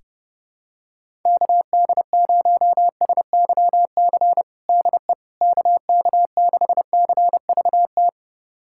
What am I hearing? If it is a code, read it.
Morse code, heard as a keyed tone, about 30 wpm, 700 Hz. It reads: KD0SYC DE KK6CVT
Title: USA Callsign DE Callsign 30wpm